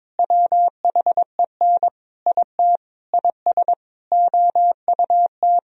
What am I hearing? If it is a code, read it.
Morse code, heard as a keyed tone, about 22 wpm, 700 Hz. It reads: WHEN IT IS OUT
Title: Sentences from Top 1000 Words - Review Part 4 - 22wpm